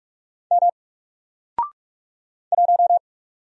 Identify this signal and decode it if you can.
Morse code, heard as a keyed tone, about 44 wpm, 700 Hz. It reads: M 1
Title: Single Letter-Number 45wpm